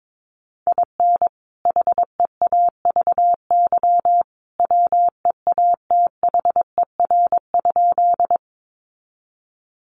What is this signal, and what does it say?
Morse code, heard as a keyed tone, about 22 wpm, 700 Hz. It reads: IN HEAVY WEATHER?
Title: Sentences from Top 1100 Words 22wpm